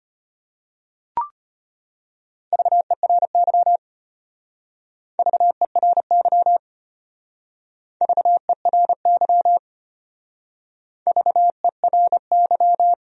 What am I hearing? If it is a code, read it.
Morse code, heard as a keyed tone, about 31 wpm, 700 Hz. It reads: VERY VERY VERY VERY
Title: Speed-Racing Top 300 Words 25wpm